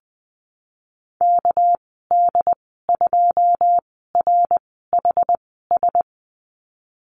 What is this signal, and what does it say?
Morse code, heard as a keyed tone, about 20 wpm, spaced as for 15 wpm, 700 Hz. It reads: KD2RHS
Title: 219 Lesson 35g - Number 0 - Call Signs 20wpm